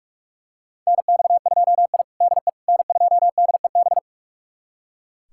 Morse code, heard as a keyed tone, about 41 wpm, 700 Hz. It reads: NX2I DE N2DEB
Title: USA Callsign DE Callsign 45wpm